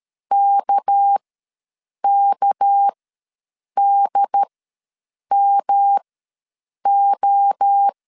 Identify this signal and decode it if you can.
Morse code, heard as a keyed tone, about 13 wpm, spaced as for 7 wpm, 800 Hz. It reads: KKDMO
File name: LEZIONI / 6 PKX0 / 6 PK ws 12wpm 3116